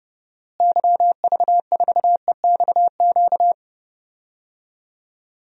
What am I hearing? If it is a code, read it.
Morse code, heard as a keyed tone, about 30 wpm, 700 Hz. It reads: YV4EXQ
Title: DXCC-100 Call Signs - 6 Characters - Repeated 2x - 30wpm